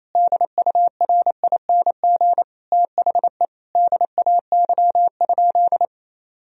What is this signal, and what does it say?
Morse code, heard as a keyed tone, about 28 wpm, 700 Hz. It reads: DURING THE DAY?
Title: Sentences from Top 1000 Words - Review Part 4 - 28wpm